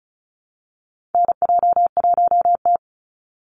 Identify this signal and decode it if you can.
Morse code, heard as a keyed tone, about 34 wpm, 700 Hz. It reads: NJ1T
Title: Call Signs Repeated 3x - 35wpm